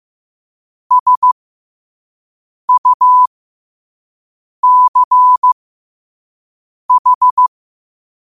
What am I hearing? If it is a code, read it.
Morse code, heard as a keyed tone, about 15 wpm, spaced as for 5 wpm, 1000 Hz. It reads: SUCH
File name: qotdfiles / 5wpm